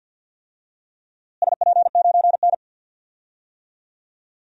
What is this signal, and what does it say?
Morse code, heard as a keyed tone, about 49 wpm, 700 Hz. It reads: SP9N
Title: DXCC-100 Call Signs - 4 Characters - Repeated 2x - 50wpm